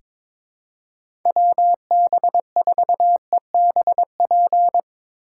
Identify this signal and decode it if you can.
Morse code, heard as a keyed tone, about 22 wpm, 700 Hz. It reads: WB4EBP
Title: USA Call Signs - 2x Character Spacing - 22wpm